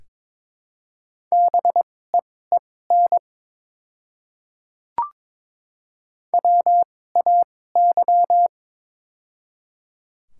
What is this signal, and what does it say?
Morse code, heard as a keyed tone, about 22 wpm, spaced as for 16 wpm, 700 Hz. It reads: BEEN WAY
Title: 2x Character Spacing - Top 200 Words - 22wpm